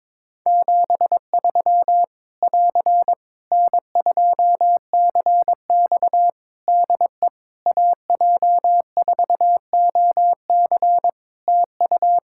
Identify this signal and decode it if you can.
Morse code, heard as a keyed tone, about 22 wpm, 700 Hz. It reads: 73 <AR> N2CX DE AJ4OC TU
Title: Practice QSO - Exchange 4 - Ending - 22wpm